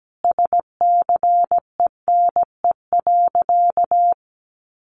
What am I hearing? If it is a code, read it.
Morse code, heard as a keyed tone, about 17 wpm, 700 Hz. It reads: SCENE.
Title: Sentences from Top 1100 Words 17wpm